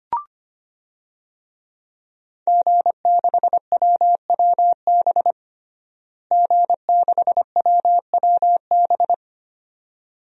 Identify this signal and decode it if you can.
Morse code, heard as a keyed tone, about 25 wpm, 700 Hz. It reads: G6WWB G6WWB
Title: DXCC-100 Call Signs - 5 Characters - Repeated 2x - 25wpm